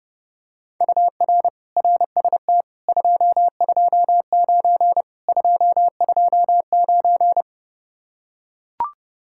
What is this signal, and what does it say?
Morse code, heard as a keyed tone, about 31 wpm, 700 Hz. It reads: UR RST 229 229